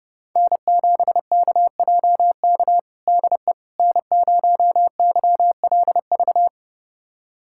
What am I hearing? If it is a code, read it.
Morse code, heard as a keyed tone, about 30 wpm, 700 Hz. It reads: N7KJK DE N0YLV
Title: USA Callsign DE Callsign - 5 Characters - 30wpm